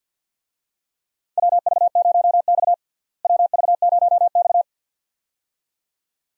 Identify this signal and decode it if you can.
Morse code, heard as a keyed tone, about 52 wpm, 700 Hz. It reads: WV0X WV0X
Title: USA Call Signs - 4 Characters - Repeated 2x - 50wpm